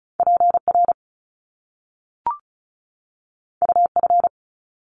Morse code, heard as a keyed tone, about 36 wpm, 700 Hz. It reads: PR UF
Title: Top Two Letter Combinations in Words 35wpm